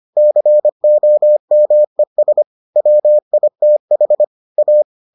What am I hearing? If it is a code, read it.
Morse code, heard as a keyed tone, about 25 wpm, 600 Hz. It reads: COMES WITH A